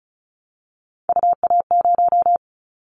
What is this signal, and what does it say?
Morse code, heard as a keyed tone, about 36 wpm, 700 Hz. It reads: UA0
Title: Callsign Prefixes for Top 100 DXCC - 35wpm